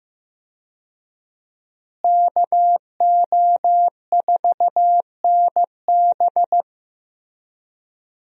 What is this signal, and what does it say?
Morse code, heard as a keyed tone, about 15 wpm, 700 Hz. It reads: KO4NB
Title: USA Call Signs - 5 Characters - Repeated 3x - 15wpm